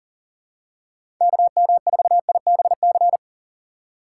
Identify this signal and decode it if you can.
Morse code, heard as a keyed tone, about 40 wpm, 690 Hz. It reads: KM4IBC
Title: USA Call Signs - 4x Character Spacing - 40wpm